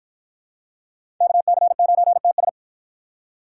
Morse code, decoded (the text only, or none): KC9TS